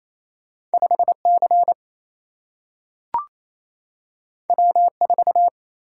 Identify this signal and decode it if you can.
Morse code, heard as a keyed tone, about 28 wpm, 700 Hz. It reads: 5 C W 4
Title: Single-Letter Single-Number 28wpm